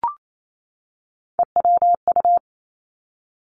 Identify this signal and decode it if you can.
Morse code, heard as a keyed tone, about 28 wpm, 700 Hz. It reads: EWU
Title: Three Challenging Letter-Number Combinations 28wpm